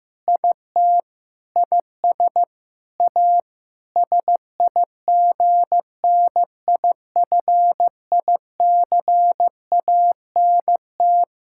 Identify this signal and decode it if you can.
Morse code, heard as a keyed tone, about 15 wpm, 700 Hz. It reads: IT IS A SIGNIFICANT
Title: Sentences from Top 1000 Words - Review Part 6 - 15wpm